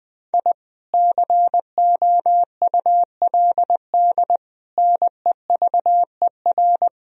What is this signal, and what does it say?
Morse code, heard as a keyed tone, about 20 wpm, 700 Hz. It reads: I COULD NEVER
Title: Sentences from Top 1000 Words - Review Part 3 - 20wpm